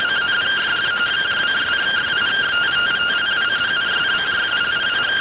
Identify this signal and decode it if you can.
Two-tone teleprinter signal, about 170 Hz shift.